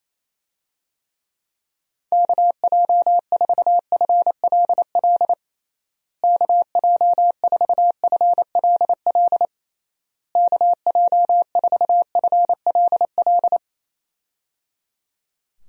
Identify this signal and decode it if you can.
Morse code, heard as a keyed tone, about 28 wpm, 700 Hz. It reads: KJ4FLL KJ4FLL KJ4FLL
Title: USA Call Signs - 6 Characters - Repeated 3x - 28wpm